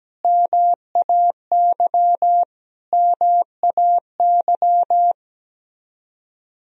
Morse code, heard as a keyed tone, about 17 wpm, 700 Hz. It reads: MAY MAY
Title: Top 100 Words Repeated 17wpm